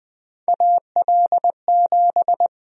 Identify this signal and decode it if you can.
Morse code, heard as a keyed tone, about 20 wpm, 700 Hz. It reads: AL7
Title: USA Call Sign Prefixes - 20wpm